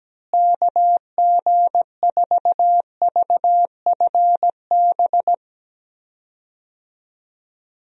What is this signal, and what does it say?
Morse code, heard as a keyed tone, about 17 wpm, 700 Hz. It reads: KG4VFB
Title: USA Call Signs - 6 Characters - Repeated 2x - 17wpm